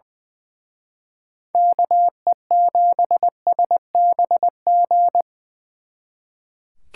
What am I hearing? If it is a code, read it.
Morse code, heard as a keyed tone, about 24 wpm, 700 Hz. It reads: KE7SBG
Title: Speed Racing - USA 2x3 Call Signs 20wpm